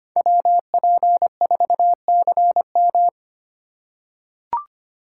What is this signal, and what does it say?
Morse code, heard as a keyed tone, about 25 wpm, 700 Hz. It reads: WP4CM
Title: USA Call Signs - 4x Character Spacing - 25wpm